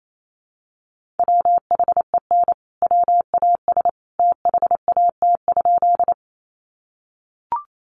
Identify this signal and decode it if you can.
Morse code, heard as a keyed tone, about 28 wpm, 700 Hz. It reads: WHEN WAS THAT?
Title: Sentences from Top 100 Words 28wpm